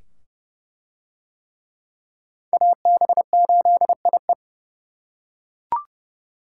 Morse code, heard as a keyed tone, about 27 wpm, 700 Hz. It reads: AB8IE